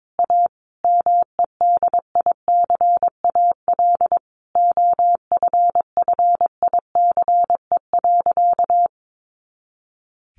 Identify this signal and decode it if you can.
Morse code, heard as a keyed tone, about 22 wpm, 700 Hz. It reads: A MEDICAL OFFICE.
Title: Additional Easy Phrases 22wpm